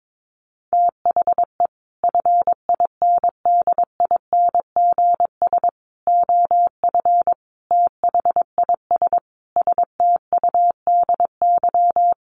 Morse code, heard as a keyed tone, about 22 wpm, 700 Hz. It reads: THE FINDINGS OF THIS STUDY
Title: Sets of 5 Words Encore2 22wpm